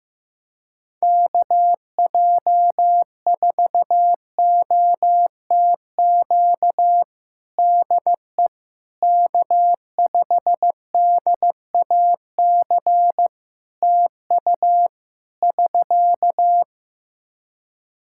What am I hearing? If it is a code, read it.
Morse code, heard as a keyed tone, about 15 wpm, 700 Hz. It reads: KJ4OTQ DE K5DAC TU <SK>
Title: Practice QSO - Exchange 4 - Ending - 15wpm